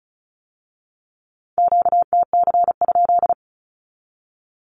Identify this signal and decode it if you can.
Morse code, heard as a keyed tone, about 36 wpm, 700 Hz. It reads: QTC?
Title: Q-Codes with Spoken Meaning - 35wpm